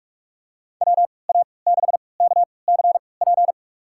Morse code, heard as a keyed tone, about 47 wpm, spaced as for 26 wpm, 700 Hz. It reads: WA6KCP
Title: USA Call Signs - 6 Characters - 3x Character Spacing - 45wpm